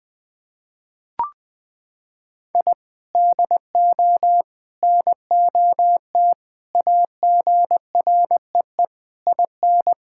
Morse code, heard as a keyed tone, about 20 wpm, 700 Hz. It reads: I DO NOT AGREE IN
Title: Sentences with 6-Word Phrases - 20wpm